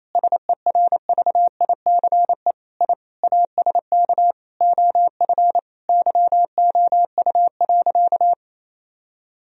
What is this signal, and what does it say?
Morse code, heard as a keyed tone, about 28 wpm, 700 Hz. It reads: SERVICE I ASK OF YOU.
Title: Sentences from Top 1000 Words - Review Part 4 - 28wpm